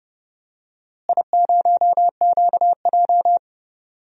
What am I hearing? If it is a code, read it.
Morse code, heard as a keyed tone, about 31 wpm, 700 Hz. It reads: I0QJ
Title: DXCC-100 Call Signs - 4 Characters - Repeated 2x - 30wpm